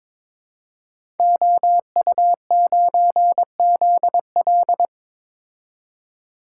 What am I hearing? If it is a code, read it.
Morse code, heard as a keyed tone, about 23 wpm, 700 Hz. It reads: OU9ZL
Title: Code Groups - 5 Letter:Numbers - 22wpm